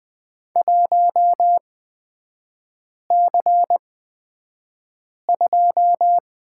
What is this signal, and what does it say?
Morse code, heard as a keyed tone, about 20 wpm, spaced as for 5 wpm, 700 Hz. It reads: 1C2
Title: 243 Lesson 39e - Letter z - Three Characters 20wpm